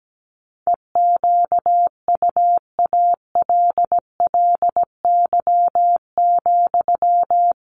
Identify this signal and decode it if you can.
Morse code, heard as a keyed tone, about 17 wpm, 700 Hz. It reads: EQUALLY,
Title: Sentences from Top 1400 Words 17wpm